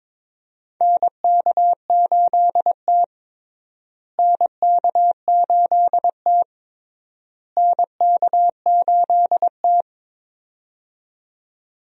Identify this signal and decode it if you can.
Morse code, heard as a keyed tone, about 22 wpm, 700 Hz. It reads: NK8T NK8T NK8T
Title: USA Call Signs - 4 Characters - Repeated 3x - 22wpm